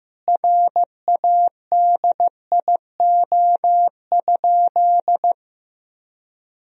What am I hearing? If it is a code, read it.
Morse code, heard as a keyed tone, about 15 wpm, 700 Hz. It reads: RADIO?
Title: Sentences from Top 1000 Words - Review Part 3 - 15wpm